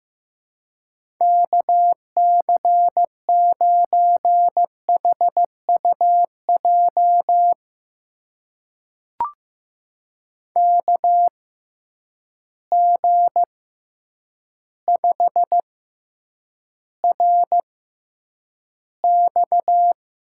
Morse code, heard as a keyed tone, about 15 wpm, spaced as for 8 wpm, 700 Hz. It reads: KC9HUJ KG5RX
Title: USA Call Signs - 6x Character Spacing - 15wpm